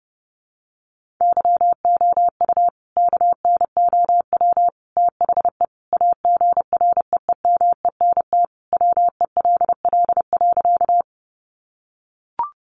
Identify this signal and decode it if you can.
Morse code, heard as a keyed tone, about 30 wpm, 700 Hz. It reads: YOU KNOW THE AGREEMENT WELL.
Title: Sentences from Top 1500 Words 30wpm